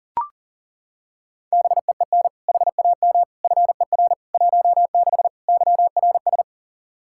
Morse code, heard as a keyed tone, about 40 wpm, 700 Hz. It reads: BEEN HAM FER 16 YRS
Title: Practice QSO - Exchange 3 - More Info - 40wpm